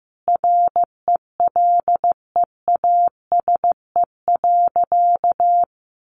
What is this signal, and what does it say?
Morse code, heard as a keyed tone, about 14 wpm, 700 Hz. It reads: RELEASE.
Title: Sentences from Top 1100 Words 15wpm